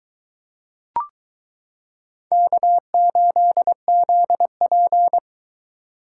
Morse code, heard as a keyed tone, about 22 wpm, 700 Hz. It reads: K8ZP